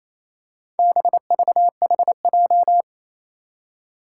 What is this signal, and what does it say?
Morse code, heard as a keyed tone, about 27 wpm, 700 Hz. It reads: BVHJ
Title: Four Challenging Letter-Number Combinations 28wpm